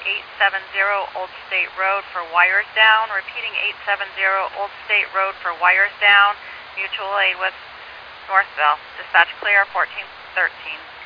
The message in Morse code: none

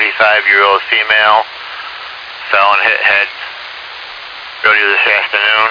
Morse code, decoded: none

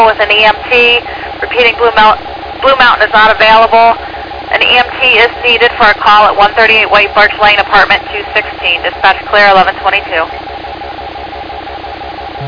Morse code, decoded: none

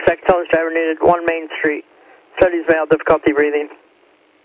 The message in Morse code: none

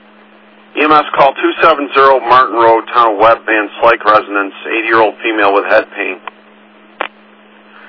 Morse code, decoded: none

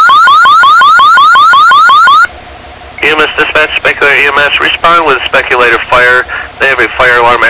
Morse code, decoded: none

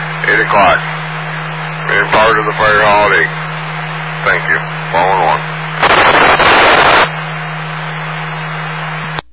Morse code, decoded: none

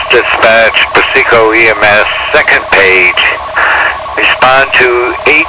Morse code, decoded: none

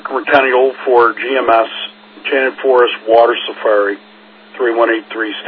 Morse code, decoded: none

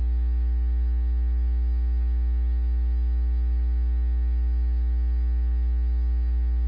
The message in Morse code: none

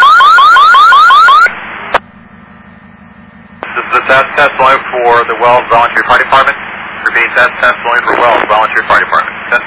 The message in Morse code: none

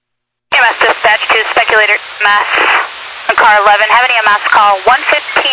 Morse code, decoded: none

